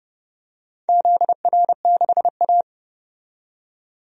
Morse code, heard as a keyed tone, about 30 wpm, 700 Hz. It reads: ZR6A
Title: DXCC-100 Call Signs - 4 Characters - 30wpm